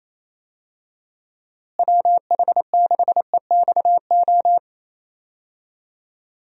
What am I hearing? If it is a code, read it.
Morse code, heard as a keyed tone, about 28 wpm, 700 Hz. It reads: WH6EXO
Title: USA Call Signs - 6 Characters - Repeated 3x - 28wpm